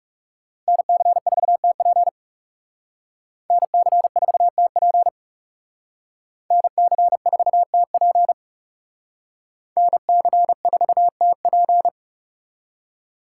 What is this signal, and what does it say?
Morse code, heard as a keyed tone, about 38 wpm, 700 Hz. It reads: NC4TP NC4TP NC4TP NC4TP